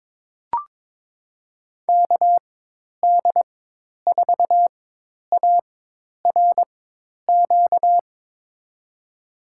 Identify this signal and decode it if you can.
Morse code, heard as a keyed tone, about 22 wpm, spaced as for 10 wpm, 700 Hz. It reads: KD4ARQ
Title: USA Call Signs - 6 Characters - 4x Character Spacing - 22wpm